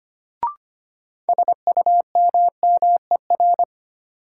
Morse code, heard as a keyed tone, about 25 wpm, 700 Hz. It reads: SUMMER
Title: Sentences from Top 1000 Words - Review Part 6 - New Word Drill - 25wpm